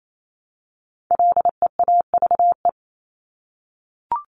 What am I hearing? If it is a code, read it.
Morse code, heard as a keyed tone, about 28 wpm, 700 Hz. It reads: LEAVE
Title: Top 500 Words Repeated 28wpm